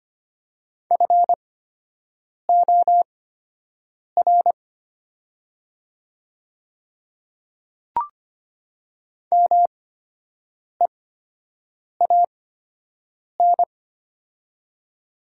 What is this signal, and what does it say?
Morse code, heard as a keyed tone, about 25 wpm, spaced as for 7 wpm, 700 Hz. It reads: FOR MEAN
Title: 8x Character Spacing - Top 200 Words - 25wpm